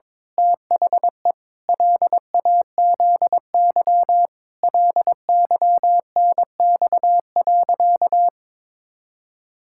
Morse code, heard as a keyed tone, about 22 wpm, 700 Hz. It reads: THE LAZY LYNX.
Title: Pangram Challenge v2 - 22wpm